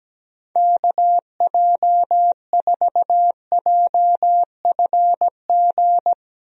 Morse code, heard as a keyed tone, about 17 wpm, 700 Hz. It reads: KJ4JFG
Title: USA Call Signs - 6 Characters - 17wpm